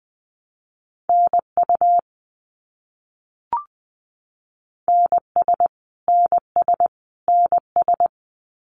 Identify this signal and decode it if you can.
Morse code, heard as a keyed tone, about 20 wpm, 700 Hz. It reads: NU NS NS NS
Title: Canadian Provinces and Territories - Repeated 3x - 20wpm